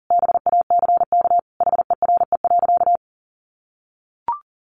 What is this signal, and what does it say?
Morse code, heard as a keyed tone, about 40 wpm, 700 Hz. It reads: BACK HERE.
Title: Sentences from Top 600 Words 40wpm